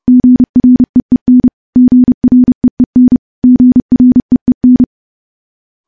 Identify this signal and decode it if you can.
Morse code, heard as a keyed tone, about 30 wpm, 250 Hz. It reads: GREEN GREEN GREEN
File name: morse / morse-training-30WPM